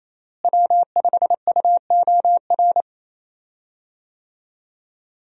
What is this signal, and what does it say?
Morse code, heard as a keyed tone, about 28 wpm, 700 Hz. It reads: W5UOR
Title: USA Call Signs - 5 Characters - Repeated 3x - 28wpm